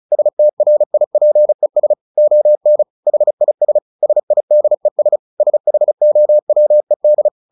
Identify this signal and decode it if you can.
Morse code, heard as a keyed tone, about 34 wpm, 600 Hz. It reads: STRIPES ON HIS SIDES SHOWED
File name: the old man and the sea / the old man and the sea 1 35wpm